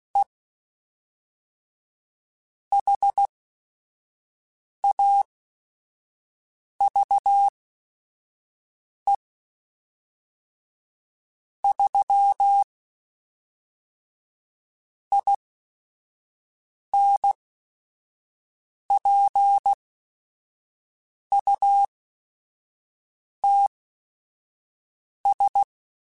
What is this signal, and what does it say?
Morse code, heard as a keyed tone, about 16 wpm, spaced as for 5 wpm, 800 Hz. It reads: E HAVE 3 INPUTS
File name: Technical Subjects / Technical Subjects-mp3s / AND Gate @5